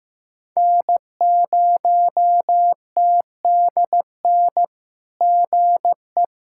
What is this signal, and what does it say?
Morse code, heard as a keyed tone, about 15 wpm, 700 Hz. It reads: N0TDN GE
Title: SOTA QSO Practice 15wpm